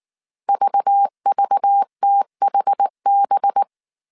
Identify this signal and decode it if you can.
Morse code, heard as a keyed tone, about 19 wpm, 800 Hz. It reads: VVTHB
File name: CW / DET / 1 6 1500C S A 18wpm 2255